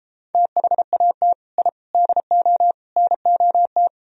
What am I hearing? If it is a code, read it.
Morse code, heard as a keyed tone, about 34 wpm, 700 Hz. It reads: THAT I DO NOT